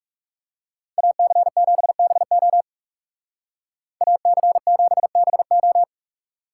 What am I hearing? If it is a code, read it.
Morse code, heard as a keyed tone, about 37 wpm, 700 Hz. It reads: AC7BO AC7BO
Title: Speed Racing - USA 2x2 Call Signs 30wpm